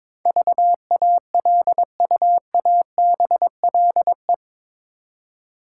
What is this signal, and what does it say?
Morse code, heard as a keyed tone, about 21 wpm, 700 Hz. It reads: VALUABLE
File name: practice / 22wpm / Eight Letter Words - 22wpm